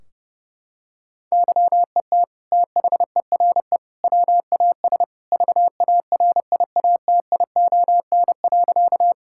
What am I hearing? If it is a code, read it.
Morse code, heard as a keyed tone, about 30 wpm, 700 Hz. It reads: YET THERE WAS VARIATION.
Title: Sentences from Top 1600 Words 30wpm